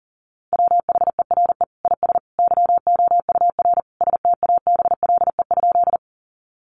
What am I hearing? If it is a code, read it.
Morse code, heard as a keyed tone, about 37 wpm, 700 Hz. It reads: WHERE IS YOUR STABLE?
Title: Sentences from Top 1500 Words 40wpm